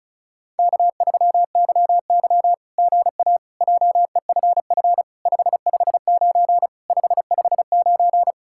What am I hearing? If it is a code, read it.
Morse code, heard as a keyed tone, about 33 wpm, 700 Hz. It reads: K3YY GA JEFF 559 559
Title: SOTA QSO Practice 35wpm